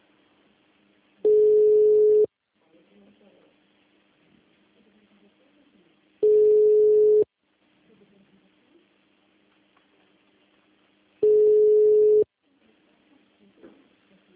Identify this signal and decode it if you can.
Ringback tone (Europe, 425 Hz), with speech.